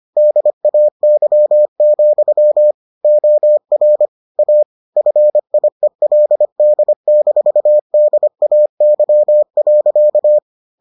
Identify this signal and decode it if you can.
Morse code, heard as a keyed tone, about 25 wpm, 600 Hz. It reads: DAY, OR A FIELD-DAY.